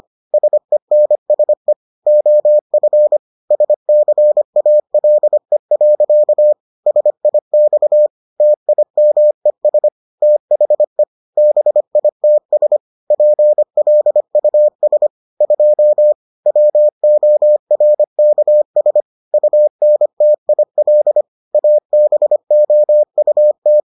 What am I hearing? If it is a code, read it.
Morse code, heard as a keyed tone, about 25 wpm, 600 Hz. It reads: SENSE OF SCALE. SIX TIMES THE BITS PLUS 2 WORKS UNTIL ABOUT